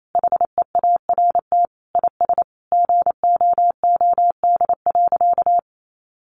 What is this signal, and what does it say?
Morse code, heard as a keyed tone, about 28 wpm, 700 Hz. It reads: HEART IS GOOD.
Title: Sentences from Top 1200 Words 28wpm